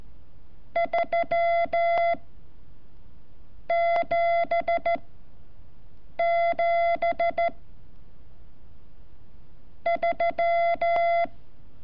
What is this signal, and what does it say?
Morse code, heard as a keyed tone, about 14 wpm, 680 Hz. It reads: 3 7 7 3
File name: curso / ejercicio 28